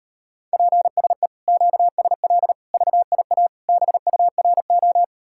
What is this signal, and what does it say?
Morse code, heard as a keyed tone, about 38 wpm, 700 Hz. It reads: PSE QSL VIA BURO